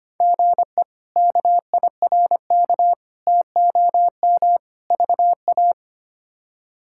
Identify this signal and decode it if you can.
Morse code, heard as a keyed tone, about 25 wpm, 700 Hz. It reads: GE KIRK TOM VA
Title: K1USN SST QSO Practice - 25wpm